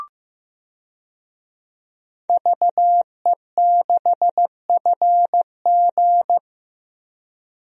Morse code, heard as a keyed tone, about 15 wpm, 700 Hz. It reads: VE6FG
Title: DXCC-100 Call Signs - 5 Characters - Repeated 3x - 15wpm